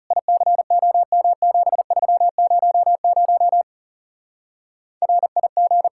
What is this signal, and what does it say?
Morse code, heard as a keyed tone, about 37 wpm, 700 Hz. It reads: ICOM7300 RIG
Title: Speed-Racing Common QSO Phrases - 30wpm